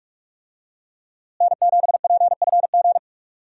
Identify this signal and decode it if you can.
Morse code, heard as a keyed tone, about 46 wpm, 700 Hz. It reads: N7PFG
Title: USA Call Signs - 5 Characters - Repeated 2x - 45wpm